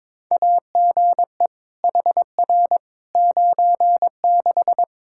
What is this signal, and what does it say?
Morse code, heard as a keyed tone, about 22 wpm, 700 Hz. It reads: AGE HR 96
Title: Practice QSO - Exchange 3 - More Info - 22wpm